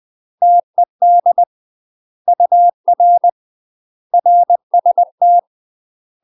Morse code, heard as a keyed tone, about 20 wpm, 700 Hz. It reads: TED UR RST
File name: UK MASTER QSOs TRAINING / 20WPM / UK MASTER QSOs 0004